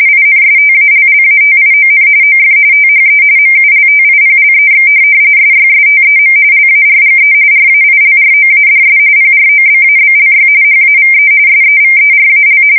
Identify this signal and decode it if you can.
RTTY, about 45.45 baud. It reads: INSTALLATION OF THE MULTIPLEX SYSTEM LARGE RAILROADS PROBABLY HAVE SEVERAL
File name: fsk170 / printing-telegraph